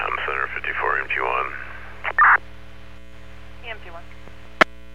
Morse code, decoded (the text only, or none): none